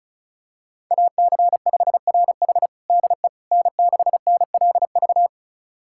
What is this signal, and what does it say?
Morse code, heard as a keyed tone, about 36 wpm, 700 Hz. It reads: AC5RH DE N6NLV
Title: USA Callsign DE Callsign 35wpm